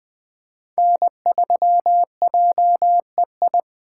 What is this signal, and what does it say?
Morse code, heard as a keyed tone, about 20 wpm, 700 Hz. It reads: N3JEI
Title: Speed Racing - USA 1x3 Call Signs 17wpm